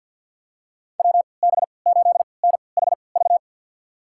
Morse code, heard as a keyed tone, about 50 wpm, spaced as for 28 wpm, 700 Hz. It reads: WB8NHV
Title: USA Call Signs - 3x Character Spacing - 50wpm